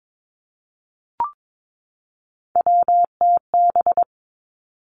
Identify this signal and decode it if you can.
Morse code, heard as a keyed tone, about 22 wpm, 700 Hz. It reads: WTB
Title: Three Challenging Letter-Number Combinations 22wpm